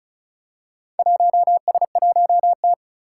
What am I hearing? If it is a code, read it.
Morse code, heard as a keyed tone, about 36 wpm, 700 Hz. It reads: 1S1T
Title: Four Challenging Letter-Number Combinations 35wpm